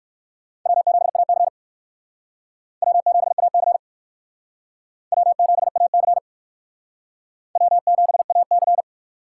Morse code, heard as a keyed tone, about 61 wpm, 700 Hz. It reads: W7AC W7AC W7AC W7AC